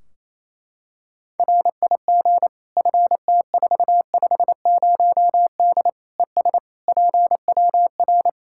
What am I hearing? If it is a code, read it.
Morse code, heard as a keyed tone, about 28 wpm, 700 Hz. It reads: RIG FT450D ES PWR
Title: K1USN SST QSO Practice 28wpm